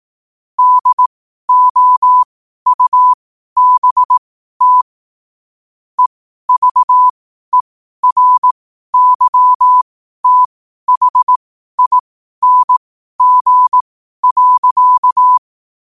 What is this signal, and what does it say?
Morse code, heard as a keyed tone, about 18 wpm, spaced as for 12 wpm, 1000 Hz. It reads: DOUBT EVERYTHING.
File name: qotdfiles / 12wpm